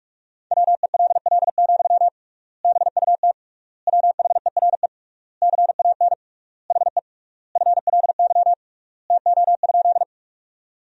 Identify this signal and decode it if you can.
Morse code, heard as a keyed tone, about 41 wpm, 700 Hz. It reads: WELL, BUT WHERE CAN HE FLY TO?
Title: 3x Word-Spacing - Sentences from Top 500 Words - 45wpm